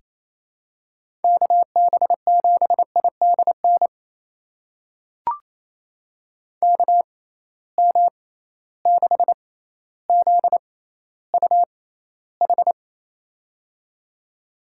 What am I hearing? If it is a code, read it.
Morse code, heard as a keyed tone, about 28 wpm, spaced as for 14 wpm, 700 Hz. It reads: KB7IDN KM6ZUH
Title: USA Call Signs - 6 Characters - 6x Character Spacing - 28wpm